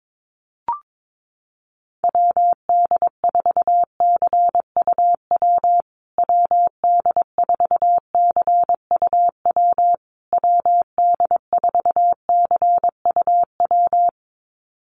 Morse code, heard as a keyed tone, about 22 wpm, 700 Hz. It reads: WD4CUW WD4CUW WD4CUW
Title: USA Call Signs Repeated 3x 22wpm